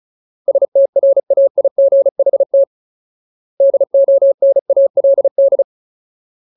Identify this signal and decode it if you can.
Morse code, heard as a keyed tone, about 34 wpm, 550 Hz. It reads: STRAIGHT DONALD
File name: Ultimate Word List / 35WPM Ultimate